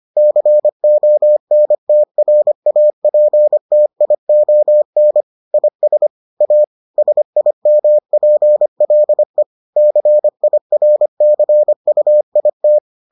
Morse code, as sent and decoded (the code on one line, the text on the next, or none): -.-. --- -. - .-. .- .--. - .. --- -. / .. ... / .- / ... .. -- .--. .-.. / -.-. .. .-. -.-. ..- .. -
CONTRAPTION IS A SIMPLE CIRCUIT